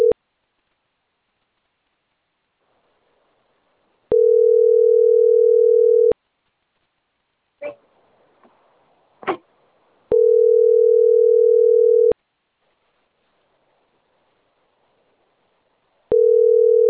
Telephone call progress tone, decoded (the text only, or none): ringback tone